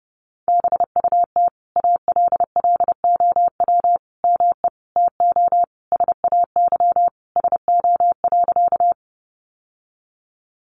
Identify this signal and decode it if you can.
Morse code, heard as a keyed tone, about 30 wpm, 700 Hz. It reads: BUT ALLOW ME TO SAY SO.
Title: Sentences from Top 700 Words 30wpm